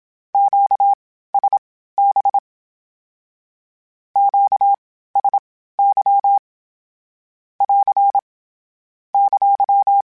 Morse code, heard as a keyed tone, about 27 wpm, spaced as for 15 wpm, 800 Hz. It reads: QSB QSY <AR> !
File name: (spaced) / Q-Calls Spaced-mp3s / Q Calls sp Once @25